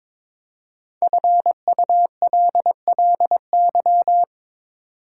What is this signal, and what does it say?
Morse code, heard as a keyed tone, about 22 wpm, 690 Hz. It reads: FULLY
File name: 5 Letter Words - First 3 Letters Given - 22wpm